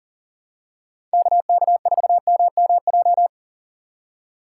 Morse code, heard as a keyed tone, about 40 wpm, 690 Hz. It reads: KK4MMJ
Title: USA Call Signs - 4x Character Spacing - 40wpm